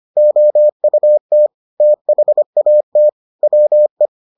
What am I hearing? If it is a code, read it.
Morse code, heard as a keyed tone, about 25 wpm, 600 Hz. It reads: OUT THAT WE